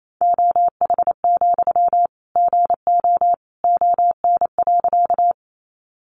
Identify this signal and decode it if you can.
Morse code, heard as a keyed tone, about 28 wpm, 700 Hz. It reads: OH, GO ON.
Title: Sentences from Top 500 Words 28wpm